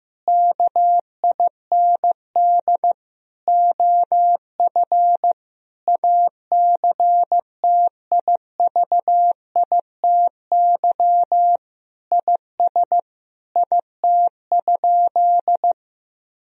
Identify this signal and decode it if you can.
Morse code, heard as a keyed tone, about 15 wpm, 700 Hz. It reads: KIND OF ACTIVITY IS IT?
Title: Sentences from Top 1000 Words - Review Part 4 - 15wpm